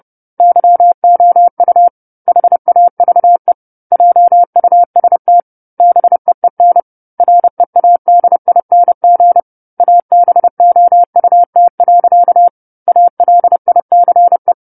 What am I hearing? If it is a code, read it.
Morse code, heard as a keyed tone, about 30 wpm, 700 Hz. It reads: YOU HAVE JUST BEEN READING ABOUT. ALICE
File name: 10-Chapter-10 30 WPM